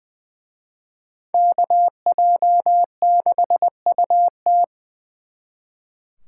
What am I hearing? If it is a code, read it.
Morse code, heard as a keyed tone, about 23 wpm, 700 Hz. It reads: KJ6UT
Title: Speed Racing - USA 2x2 Call Signs 20wpm